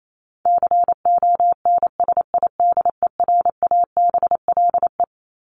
Morse code, heard as a keyed tone, about 28 wpm, 700 Hz. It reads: CONSIDERABLE
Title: LY Suffix - Top 100 Words with Base Words 28wpm